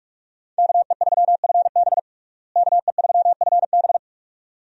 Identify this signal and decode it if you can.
Morse code, heard as a keyed tone, about 47 wpm, 700 Hz. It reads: KE3FB KE3FB
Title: USA Call Signs - 5 Characters - Repeated 2x - 45wpm